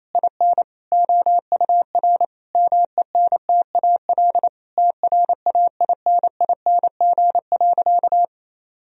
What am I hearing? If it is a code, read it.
Morse code, heard as a keyed tone, about 28 wpm, 700 Hz. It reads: IN OUR MENTAL TRAINING.
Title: Sentences from Top 1000 Words - Review Part 4 - 28wpm